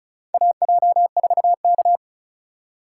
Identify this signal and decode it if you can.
Morse code, heard as a keyed tone, about 36 wpm, 700 Hz. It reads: AJ4K
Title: USA Call Signs - 4 Characters - 3x Character Spacing - 35wpm